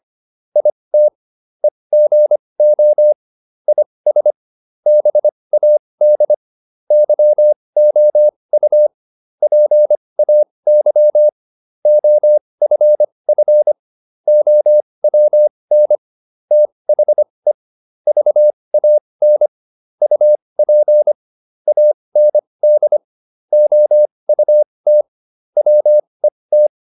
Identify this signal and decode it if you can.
Morse code, heard as a keyed tone, about 25 wpm, spaced as for 20 wpm, 600 Hz. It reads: IT EGO IS BAD YOU PAY OFF OWN THE VAN UP AND OUT WET